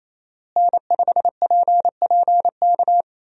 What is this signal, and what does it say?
Morse code, heard as a keyed tone, about 27 wpm, 700 Hz. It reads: N5PPK
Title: USA Call Signs - 5 Characters - Repeated 3x - 28wpm